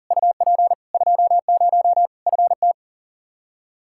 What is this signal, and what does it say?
Morse code, heard as a keyed tone, about 40 wpm, 700 Hz. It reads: UP 20 FT
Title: Practice QSO - Exchange 2 - Info - 40wpm